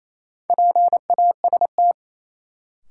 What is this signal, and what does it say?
Morse code, heard as a keyed tone, about 28 wpm, 700 Hz. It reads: PAST
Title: Sentences from Top 600 Words - New Word Drill - 28wpm